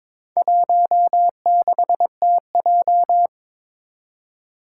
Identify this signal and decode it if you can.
Morse code, heard as a keyed tone, about 21 wpm, 700 Hz. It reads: 16TJ
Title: Four Challenging Letter-Number Combinations 22wpm